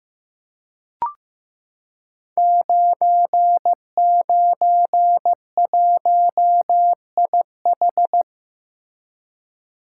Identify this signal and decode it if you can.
Morse code, heard as a keyed tone, about 15 wpm, 700 Hz. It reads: 991IH
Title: Code Groups - 5 Letter:Numbers - 15wpm